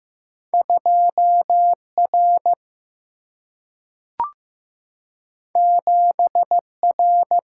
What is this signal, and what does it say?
Morse code, heard as a keyed tone, about 15 wpm, 700 Hz. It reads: 2R 7R
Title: Single-Letter Single-Number 15wpm